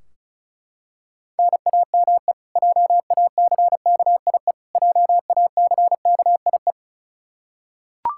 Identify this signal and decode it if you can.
Morse code, heard as a keyed tone, about 36 wpm, 700 Hz. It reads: NAME JACKIE JACKIE
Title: Practice QSO - Exchange 1 - Beginning - 35wpm